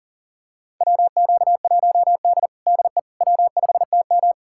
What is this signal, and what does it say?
Morse code, heard as a keyed tone, about 40 wpm, 700 Hz. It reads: WQ1D DE W5TM
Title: USA Callsign DE Callsign - 4 Characters - 40wpm